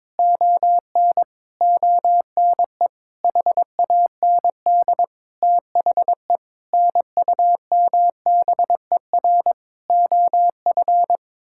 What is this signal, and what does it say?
Morse code, heard as a keyed tone, about 22 wpm, 700 Hz. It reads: ON ONE HAND THE NUMBER OF
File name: practice / 22wpm / Sets of 8 Words - 22wpm